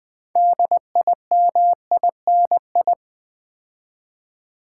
Morse code, heard as a keyed tone, about 20 wpm, 700 Hz. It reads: DIMINI
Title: Build A Word - 3 Letters At A Time - 8-Letter Words - 20wpm